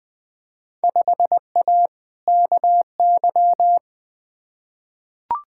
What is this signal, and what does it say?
Morse code, heard as a keyed tone, about 20 wpm, 700 Hz. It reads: 5A KY
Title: Field Day Exchanges - 20wpm